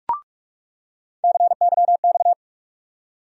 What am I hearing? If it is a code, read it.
Morse code, heard as a keyed tone, about 43 wpm, 700 Hz. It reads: CYX
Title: USA Call Sign Postfixes - 3 Letters 45wpm